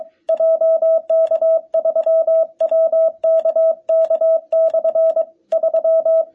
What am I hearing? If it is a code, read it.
Morse code, heard as a keyed tone, about 23 wpm, 650 Hz. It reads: JK3WKK/3
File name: jh8cbh / 2021cw